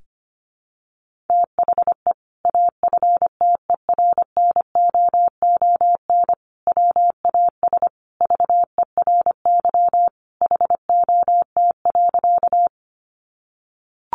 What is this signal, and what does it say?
Morse code, heard as a keyed tone, about 25 wpm, 700 Hz. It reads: THE AFTERNOON WAS VERY HOT.
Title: Sentences from Top 1500 Words 25wpm